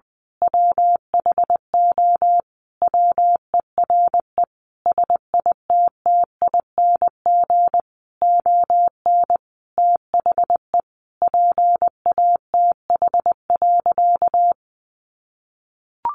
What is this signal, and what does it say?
Morse code, heard as a keyed tone, about 20 wpm, 700 Hz. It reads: WHO WERE SITTING ON THE PATH.
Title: A Sound Of Thunder 20wpm